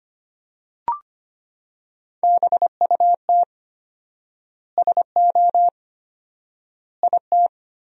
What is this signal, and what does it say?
Morse code, heard as a keyed tone, about 25 wpm, 700 Hz. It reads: BUT SO IT
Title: 4x Word-Spacing - Sentences from Top 100 Words - 25wpm